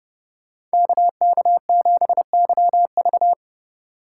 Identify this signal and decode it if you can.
Morse code, heard as a keyed tone, about 34 wpm, 700 Hz. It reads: KK7YV